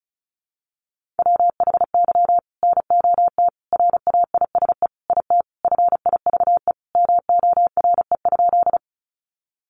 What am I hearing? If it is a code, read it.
Morse code, heard as a keyed tone, about 34 wpm, 700 Hz. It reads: WHY NOT RAISE IT FIVE MORE?
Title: Sentences from Top 1400 Words 35wpm